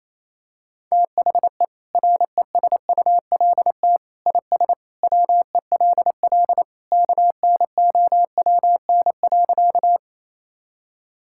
Morse code, heard as a keyed tone, about 28 wpm, 700 Hz. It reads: THE RESULT IS WELL KNOWN.
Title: Sentences from Top 1000 Words - Review Part 5 - 28wpm